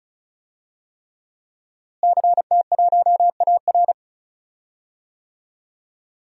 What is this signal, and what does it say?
Morse code, heard as a keyed tone, about 36 wpm, 700 Hz. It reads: CT1AR
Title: DXCC-100 Call Signs - 5 Characters - Repeated 3x - 35wpm